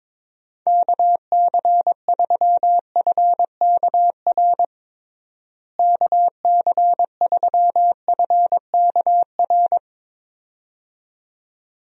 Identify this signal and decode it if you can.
Morse code, heard as a keyed tone, about 22 wpm, 700 Hz. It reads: KC3FKR KC3FKR
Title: USA Call Signs - 6 Characters - Repeated 2x - 22wpm